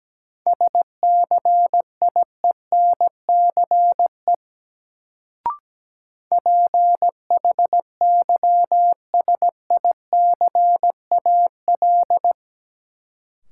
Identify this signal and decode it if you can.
Morse code, heard as a keyed tone, about 17 wpm, 700 Hz. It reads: SCIENCE PHYSICAL
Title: Sentences from Top 1400 Words - New Word Drill Alt - 17wpm